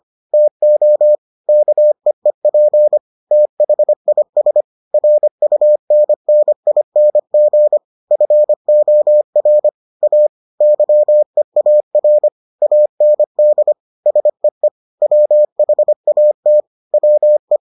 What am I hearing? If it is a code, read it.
Morse code, heard as a keyed tone, about 25 wpm, 600 Hz. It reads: TO KEEP THIS RUNNING FOR A YEAR AND SEE WHAT WE